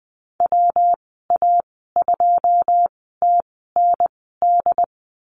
Morse code, heard as a keyed tone, about 20 wpm, spaced as for 15 wpm, 700 Hz. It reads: WA2TND
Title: 167 Lesson 27g - Number 7 - Call Signs 20wpm